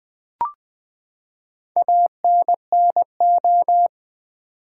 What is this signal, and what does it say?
Morse code, heard as a keyed tone, about 20 wpm, 700 Hz. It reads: ANNO
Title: Build A Word - 4 Letters At A Time - 12-Letter Words - 20wpm